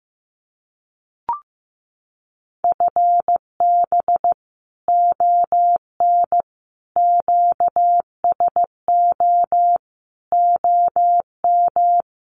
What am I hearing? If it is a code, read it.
Morse code, heard as a keyed tone, about 15 wpm, 700 Hz. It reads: FB ON QSO OM
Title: Common QSO Phrases 15wpm